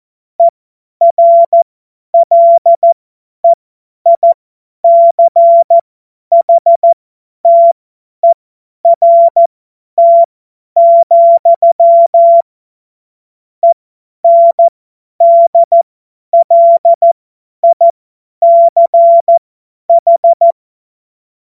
Morse code, heard as a keyed tone, about 14 wpm, spaced as for 10 wpm, 680 Hz. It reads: ERLEICHTERT, ENDLICH
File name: Morsekurs / Witze / Witz 25. Oktober 2022 50 F70